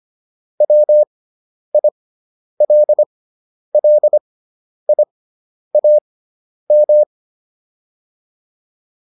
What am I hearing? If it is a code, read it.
Morse code, heard as a keyed tone, about 25 wpm, spaced as for 10 wpm, 600 Hz. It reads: WILLIAM